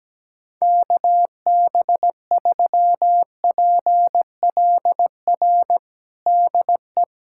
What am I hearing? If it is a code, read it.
Morse code, heard as a keyed tone, about 17 wpm, 700 Hz. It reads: KB3PLR DE
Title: USA Callsign DE Callsign 17wpm